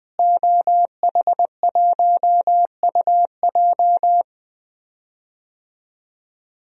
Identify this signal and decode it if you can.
Morse code, heard as a keyed tone, about 20 wpm, 700 Hz. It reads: OH1UJ
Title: DXCC-100 Call Signs - 5 Characters - Repeated 3x - 20wpm